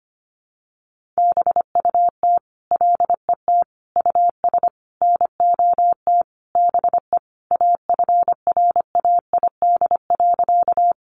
Morse code, heard as a keyed tone, about 25 wpm, 700 Hz. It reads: BUT LET US NOT BE AFRAID.
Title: Sentences from Top 1500 Words 25wpm